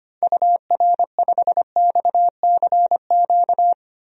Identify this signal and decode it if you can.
Morse code, heard as a keyed tone, about 25 wpm, 700 Hz. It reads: UR5XCQ
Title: DXCC-100 Call Signs - 6 Characters - 25wpm